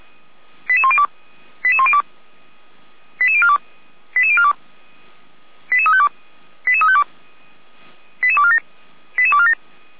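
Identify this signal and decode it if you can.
Five-tone selective call (ZVEI), heard as quick runs of five tones, tones of about 70 ms. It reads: 88102 88102 88042 88042 88352 88352 88247 88247
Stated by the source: ZVEI five-tone selective call 89376 received on an SDR.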